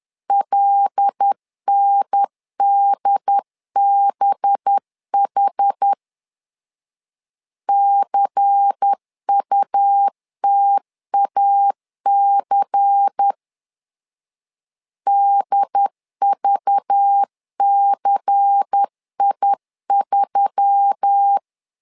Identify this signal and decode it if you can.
Morse code, heard as a keyed tone, about 10 wpm, 800 Hz. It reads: LNDBH CUTAC DVCI3
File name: LEZIONI / 5 LBVC / 5 LBVC 10wpm 2349